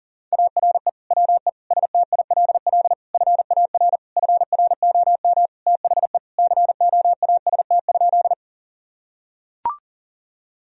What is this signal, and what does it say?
Morse code, heard as a keyed tone, about 40 wpm, 700 Hz. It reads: ARE WE STILL FAR FROM THE COAST?
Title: Sentences from Top 1000 Words - Review Part 4 - 40wpm